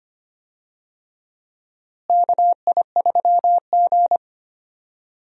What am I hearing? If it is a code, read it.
Morse code, heard as a keyed tone, about 25 wpm, 700 Hz. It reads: KI3G
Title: USA Call Signs - 4 Characters - Repeated 3x - 25wpm